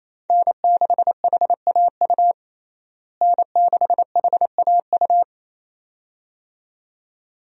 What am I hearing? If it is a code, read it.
Morse code, heard as a keyed tone, about 28 wpm, 700 Hz. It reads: N6HAU N6HAU
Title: USA Call Signs - 5 Characters - Repeated 2x - 28wpm